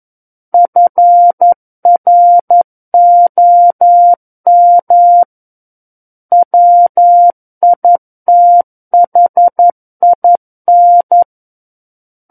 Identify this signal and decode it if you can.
Morse code, heard as a keyed tone, about 11 wpm, 700 Hz. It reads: FROM WITHIN
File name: Book One / 02-Chapter 2 - 11 WPM